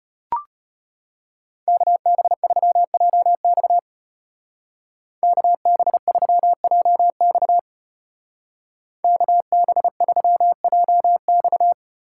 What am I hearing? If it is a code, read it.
Morse code, heard as a keyed tone, about 30 wpm, 700 Hz. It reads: KB3JX KB3JX KB3JX
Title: Speed Racing - USA 2x2 Call Signs 25wpm